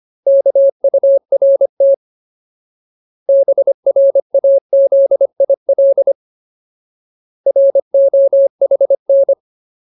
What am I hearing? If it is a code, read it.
Morse code, heard as a keyed tone, about 25 wpm, 550 Hz. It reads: KURT BRAZIL ROHN
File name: Ultimate Word List / 25WPM Ultimate